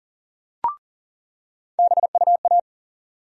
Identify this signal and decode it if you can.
Morse code, heard as a keyed tone, about 40 wpm, 700 Hz. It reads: BUA
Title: USA Call Sign Postfixes - 3 Letters 40wpm